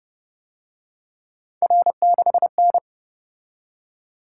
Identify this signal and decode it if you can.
Morse code, heard as a keyed tone, about 30 wpm, 700 Hz. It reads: R6N